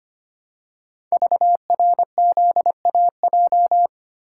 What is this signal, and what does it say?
Morse code, heard as a keyed tone, about 25 wpm, 700 Hz. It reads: VRZAJ
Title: Code Groups - 5 Letters - 25wpm